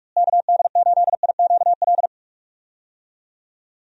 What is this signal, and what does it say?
Morse code, heard as a keyed tone, about 46 wpm, 700 Hz. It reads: KD8IQL
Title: USA Call Signs - 6 Characters - Repeated 3x - 45wpm